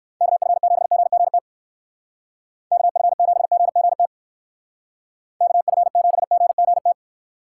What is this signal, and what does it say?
Morse code, heard as a keyed tone, about 59 wpm, 700 Hz. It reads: KF7GGT KF7GGT KF7GGT